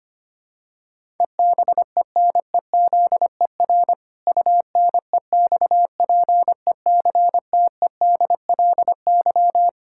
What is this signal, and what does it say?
Morse code, heard as a keyed tone, about 25 wpm, 700 Hz. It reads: EBENEZER UNEXPECTEDLY
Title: Pangram Challenge v2 - 25wpm